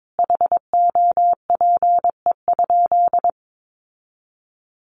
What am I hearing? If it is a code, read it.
Morse code, heard as a keyed tone, about 22 wpm, 700 Hz. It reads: HOPE?
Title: Sentences from Top 1300 Words 22wpm